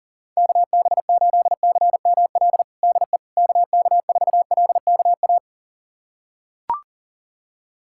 Morse code, heard as a keyed tone, about 40 wpm, 700 Hz. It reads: KB8CML DE KK4LKA
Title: USA Callsign DE Callsign 40wpm